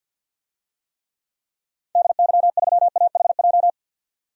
Morse code, heard as a keyed tone, about 52 wpm, 700 Hz. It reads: DY3AHJ
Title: DXCC-100 Call Signs - 6 Characters - 50wpm